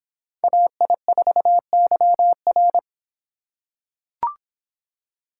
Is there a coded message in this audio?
Yes: Morse code from 0.4 to 2.8 s, heard as a keyed tone, about 26 wpm, 700 Hz. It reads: AI4YR